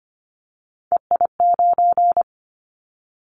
Morse code, heard as a keyed tone, about 25 wpm, 700 Hz. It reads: EI9
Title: Callsign Prefixes for Top 50 DXCC - 25wpm